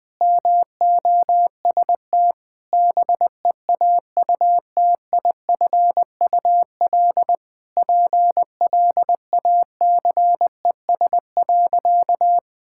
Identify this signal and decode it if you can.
Morse code, heard as a keyed tone, about 19 wpm, 700 Hz. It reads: MOST BEAUTIFUL PLACES.
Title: Sentences with 5-Word Phrases - 20wpm